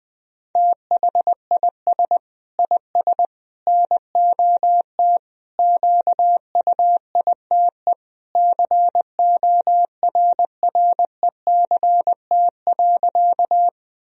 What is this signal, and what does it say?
Morse code, heard as a keyed tone, about 20 wpm, 700 Hz. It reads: THIS IS NOT QUITE CORRECT.
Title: Sentences from Top 1000 Words - Review Part 2 - 20wpm